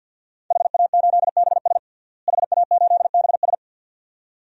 Morse code, heard as a keyed tone, about 45 wpm, 700 Hz. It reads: HA8BS HA8BS
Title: DXCC-100 Call Signs - 5 Characters - Repeated 2x - 50wpm